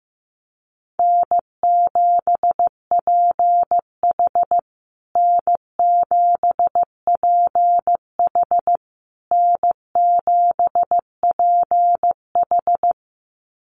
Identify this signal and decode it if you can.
Morse code, heard as a keyed tone, about 15 wpm, 700 Hz. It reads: N7PH N7PH N7PH
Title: USA Call Signs Repeated 3x 15wpm